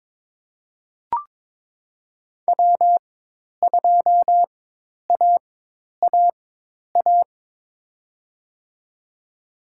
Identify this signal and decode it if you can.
Morse code, heard as a keyed tone, about 22 wpm, spaced as for 10 wpm, 700 Hz. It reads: W2AAA
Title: USA Call Signs - 5 Characters - 4x Character Spacing - 22wpm